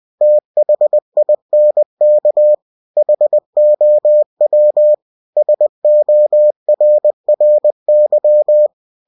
Morse code, heard as a keyed tone, about 20 wpm, 600 Hz. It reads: THINK HOW SORRY
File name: cw-books / winnie-the-pooh / winnie-20wpm-page-0072